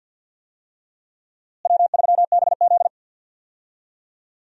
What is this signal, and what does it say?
Morse code, heard as a keyed tone, about 51 wpm, 700 Hz. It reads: W3BZ
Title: USA Call Signs - 4 Characters - Repeated 3x - 50wpm